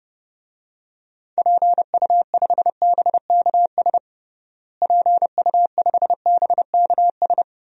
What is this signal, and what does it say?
Morse code, heard as a keyed tone, about 30 wpm, 700 Hz. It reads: PU5BKS PU5BKS
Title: DXCC-100 Call Signs - 6 Characters - Repeated 3x - 30wpm